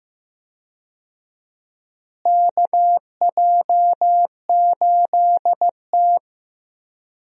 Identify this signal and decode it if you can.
Morse code, heard as a keyed tone, about 15 wpm, 700 Hz. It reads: KJ8T
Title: USA Call Signs - 4 Characters - Repeated 2x - 15wpm